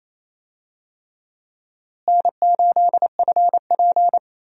Morse code, heard as a keyed tone, about 28 wpm, 700 Hz. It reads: N8FP
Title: USA Call Signs - 4 Characters - Repeated 3x - 28wpm